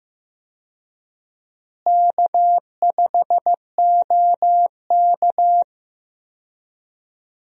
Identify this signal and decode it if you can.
Morse code, heard as a keyed tone, about 15 wpm, 700 Hz. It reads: K5OK
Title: USA Call Signs - 4 Characters - Repeated 2x - 15wpm